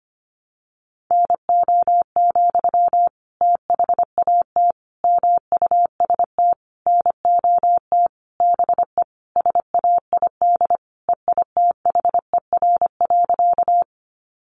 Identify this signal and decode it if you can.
Morse code, heard as a keyed tone, about 25 wpm, 700 Hz. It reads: NO, THAT MUST NOT BE SAID EITHER.
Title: Sentences from Top 600 Words 25wpm